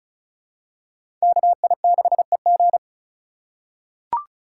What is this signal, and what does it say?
Morse code, heard as a keyed tone, about 36 wpm, 700 Hz. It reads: KI6EG
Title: USA Call Signs - 4x Character Spacing - 35wpm